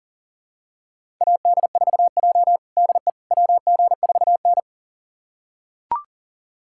Morse code, heard as a keyed tone, about 37 wpm, 700 Hz. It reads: AD4J DE WG4N
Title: USA Callsign DE Callsign - 4 Characters - 40wpm